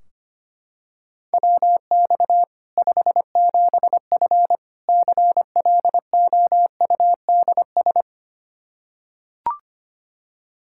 Morse code, heard as a keyed tone, about 25 wpm, 700 Hz. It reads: WX 57F CLOUDS
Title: K1USN SST QSO Practice 25wpm